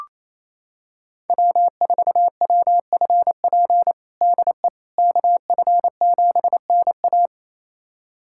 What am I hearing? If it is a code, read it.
Morse code, heard as a keyed tone, about 28 wpm, 700 Hz. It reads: W4WFP DE KF7NA
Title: USA Callsign DE Callsign - 5 Characters - 28wpm